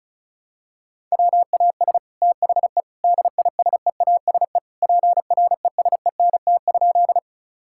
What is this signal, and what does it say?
Morse code, heard as a keyed tone, about 34 wpm, 700 Hz. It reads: WAS THE DISEASE PRESENT?
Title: Sentences from Top 1000 Words - Review Part 6 - 35wpm